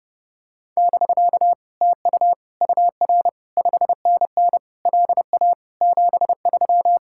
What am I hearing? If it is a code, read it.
Morse code, heard as a keyed tone, about 32 wpm, 700 Hz. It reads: <BK> TU UR 5NN LA 73